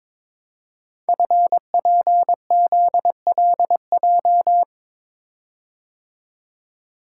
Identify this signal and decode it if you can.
Morse code, heard as a keyed tone, about 23 wpm, 700 Hz. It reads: FPZLJ
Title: Code Groups - 5 Letter:Numbers - 22wpm